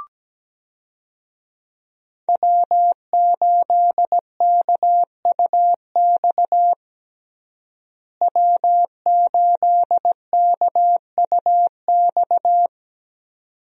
Morse code, heard as a keyed tone, about 17 wpm, 700 Hz. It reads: W8KUX W8KUX
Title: USA Call Signs - 5 Characters - Repeated 2x - 17wpm